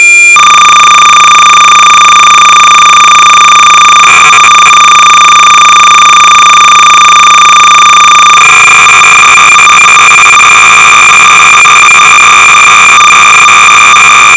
Cassette data (Kansas City type).
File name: sorcerer / 300baud / Plot-(1979)(Dick Tolomei)(Quality Software)-BASIC-Sorcerer-300baud